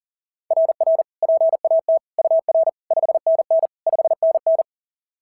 Morse code, heard as a keyed tone, about 44 wpm, 690 Hz. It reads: RR PAT UR 5NN 5NN
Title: SOTA QSO Practice 40wpm